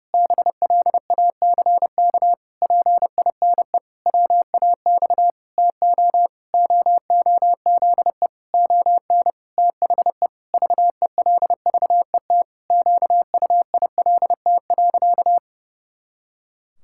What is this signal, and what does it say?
Morse code, heard as a keyed tone, about 30 wpm, 700 Hz. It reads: BLACK PINE WAX TO OOZE ON THE VELVET QUILT.
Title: Pangram Challenge v2 - 30wpm